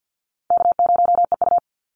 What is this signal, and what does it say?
Morse code, heard as a keyed tone, about 50 wpm, 700 Hz. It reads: K0EU
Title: Easy Call Signs - 50wpm